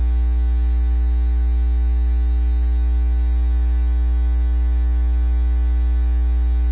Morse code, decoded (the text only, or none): none